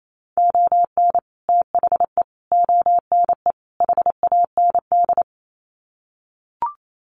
Morse code, heard as a keyed tone, about 28 wpm, 700 Hz. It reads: ON THE ONE HAND
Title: Sets of 4 Words 28wpm